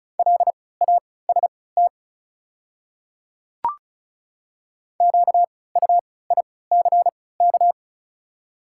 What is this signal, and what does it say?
Morse code, heard as a keyed tone, about 36 wpm, spaced as for 20 wpm, 700 Hz. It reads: LAST QUICK
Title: 3x Character Spacing - Top 500 Words - 35wpm